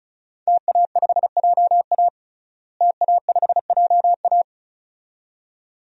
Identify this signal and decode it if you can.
Morse code, heard as a keyed tone, about 36 wpm, 700 Hz. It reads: TA5JA TA5JA
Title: DXCC-100 Call Signs - 5 Characters - Repeated 2x - 35wpm